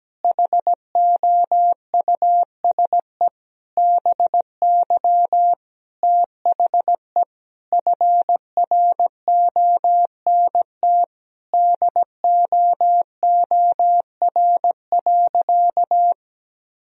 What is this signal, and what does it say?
Morse code, heard as a keyed tone, about 17 wpm, 700 Hz. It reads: HOUSE BY THE FRONT DOOR.
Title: Sentences from Top 1000 Words - Review Part 2 - 17wpm